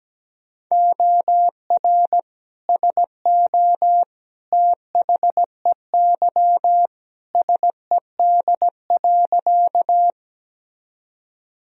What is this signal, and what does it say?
Morse code, heard as a keyed tone, about 17 wpm, 700 Hz. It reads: OR SO THEY SED.
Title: Sentences from Top 100 Words and Abbreviations 17wpm